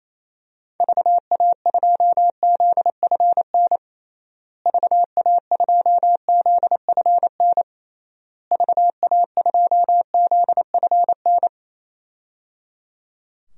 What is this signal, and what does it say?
Morse code, heard as a keyed tone, about 28 wpm, 700 Hz. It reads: VA2ZFN VA2ZFN VA2ZFN
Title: DXCC-100 Call Signs - 6 Characters - Repeated 3x - 28wpm